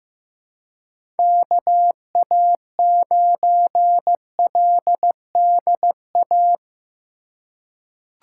Morse code, heard as a keyed tone, about 15 wpm, 700 Hz. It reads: KA9LDA
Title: USA Call Signs - 6 Characters - 4x Character Spacing - 15wpm